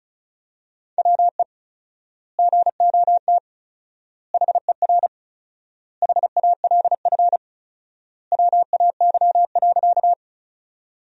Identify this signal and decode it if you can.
Morse code, heard as a keyed tone, about 34 wpm, 700 Hz. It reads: WE GOT HER HALF WAY.
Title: 4x Word-Spacing - Sentences from Top 400 Words - 35wpm